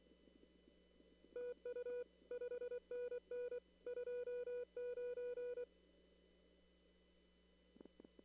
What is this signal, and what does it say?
Morse code, heard as a keyed tone, about 24 wpm, 490 Hz. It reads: TU 5NN 29
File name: CW / kt5j wpx cw-op-k5pi